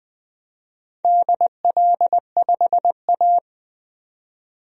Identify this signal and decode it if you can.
Morse code, heard as a keyed tone, about 20 wpm, 700 Hz. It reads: DL5A
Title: DXCC-100 Call Signs - 4 Characters - 20wpm